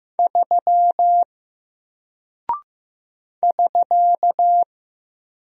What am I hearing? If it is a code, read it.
Morse code, heard as a keyed tone, about 15 wpm, 700 Hz. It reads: 3 <SK>
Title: Single Character (Letter:Number:Punctuation:Prosign) Alt 15wpm